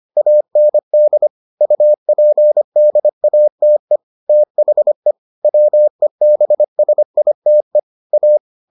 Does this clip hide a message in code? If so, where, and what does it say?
Yes: Morse code from 0.2 to 8.4 s, heard as a keyed tone, about 25 wpm, 600 Hz. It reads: AND UPDATE THE WEBSITE A